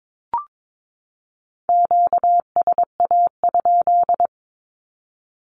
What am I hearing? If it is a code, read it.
Morse code, heard as a keyed tone, about 22 wpm, 700 Hz. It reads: QSA?
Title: Q-Codes 22wpm